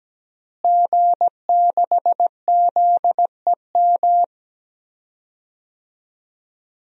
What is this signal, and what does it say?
Morse code, heard as a keyed tone, about 17 wpm, 700 Hz. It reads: G6ZEM
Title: DXCC-100 Call Signs - 5 Characters - Repeated 3x - 17wpm